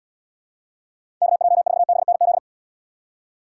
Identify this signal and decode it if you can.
Morse code, heard as a keyed tone, about 69 wpm, 700 Hz. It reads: KJ4BNZ